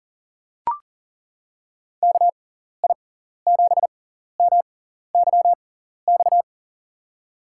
Morse code, heard as a keyed tone, about 39 wpm, spaced as for 14 wpm, 700 Hz. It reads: KI7MYX
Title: USA Call Signs - 6 Characters - 6x Character Spacing - 40wpm